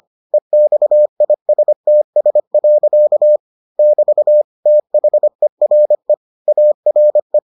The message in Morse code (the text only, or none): EXISTS. <BT> THERE ARE